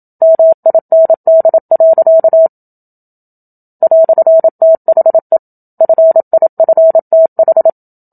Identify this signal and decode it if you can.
Morse code, heard as a keyed tone, about 27 wpm, 650 Hz. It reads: MIND. "THE FIFTH
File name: Book Two / 01-Chapter 1 - 27 WPM